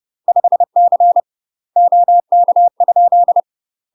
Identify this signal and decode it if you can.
Morse code, heard as a keyed tone, about 30 wpm, 700 Hz. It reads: 5C OK?
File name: UK MASTER QSOs TRAINING / 30WPM / UK MASTER QSOs 0030